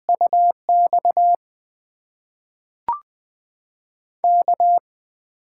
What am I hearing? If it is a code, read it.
Morse code, heard as a keyed tone, about 20 wpm, spaced as for 5 wpm, 700 Hz. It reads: UXK
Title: USA Call Signs - 5 Characters - 8x Character Spacing - 20wpm